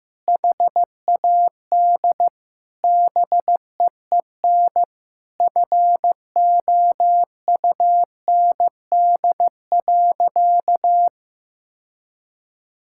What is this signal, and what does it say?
Morse code, heard as a keyed tone, about 16 wpm, 700 Hz. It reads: HAD BEEN FOUND.
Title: Sentences from Top 1600 Words 15wpm